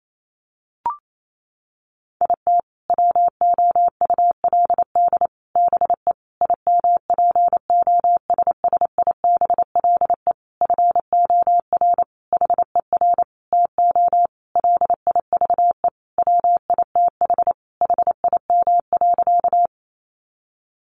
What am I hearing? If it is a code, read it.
Morse code, heard as a keyed tone, about 28 wpm, 700 Hz. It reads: IT WOULD BE IMPOSSIBLE FOR HER TO LIVE WITH HIM.
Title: Sentences from Top 1100 Words 28wpm